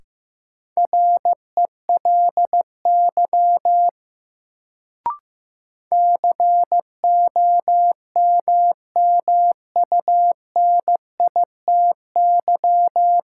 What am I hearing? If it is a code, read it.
Morse code, heard as a keyed tone, about 15 wpm, 700 Hz. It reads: RELY COMMUNITY
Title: Sentences from Top 1200 Words - New Word Drill Alt - 15wpm